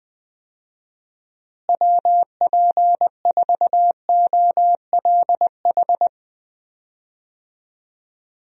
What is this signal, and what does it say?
Morse code, heard as a keyed tone, about 20 wpm, 700 Hz. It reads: WP4OLH
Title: USA Call Signs - 6 Characters - Repeated 3x - 20wpm